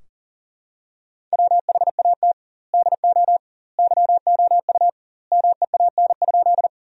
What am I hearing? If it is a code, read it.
Morse code, heard as a keyed tone, about 40 wpm, 700 Hz. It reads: WHAT DO YOU MEAN?
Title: 2x Word-Spacing - Sentences from Top 200 Words - 40wpm